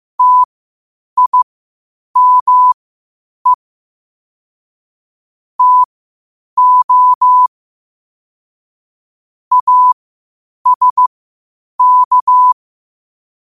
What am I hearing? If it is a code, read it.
Morse code, heard as a keyed tone, about 15 wpm, spaced as for 8 wpm, 1000 Hz. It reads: TIME TO ASK
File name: qotdfiles / 8wpm